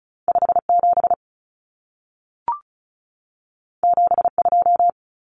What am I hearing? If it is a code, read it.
Morse code, heard as a keyed tone, about 36 wpm, 700 Hz. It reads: 57 72
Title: Common Cut Numbers - 2-Digit 35wpm